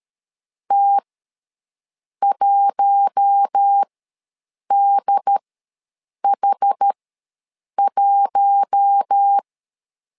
Morse code, heard as a keyed tone, about 13 wpm, spaced as for 7 wpm, 800 Hz. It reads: T1DH1
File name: LEZIONI / 4 Q1 ws 12wpm 2129